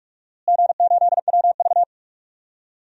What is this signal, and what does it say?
Morse code, heard as a keyed tone, about 46 wpm, 700 Hz. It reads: G8WV
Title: DXCC-100 Call Signs - 4 Characters - Repeated 3x - 45wpm